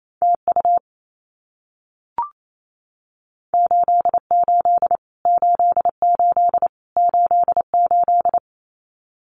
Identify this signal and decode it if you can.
Morse code, heard as a keyed tone, about 28 wpm, 700 Hz. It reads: TU 88 88 88
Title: Abbreviations Repeated 3x - 28wpm